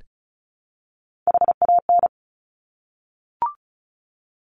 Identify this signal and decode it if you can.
Morse code, heard as a keyed tone, about 36 wpm, 700 Hz. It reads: HAN
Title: Top Three Letter Combinations in Words 35wpm